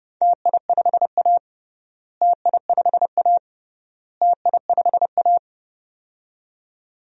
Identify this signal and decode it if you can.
Morse code, heard as a keyed tone, about 30 wpm, 700 Hz. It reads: TI5A TI5A TI5A
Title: DXCC-100 Call Signs - 4 Characters - Repeated 3x - 30wpm